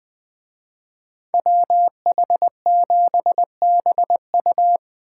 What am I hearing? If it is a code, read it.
Morse code, heard as a keyed tone, about 24 wpm, 700 Hz. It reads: WH7BU
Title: Speed Racing - USA Call Signs 20wpm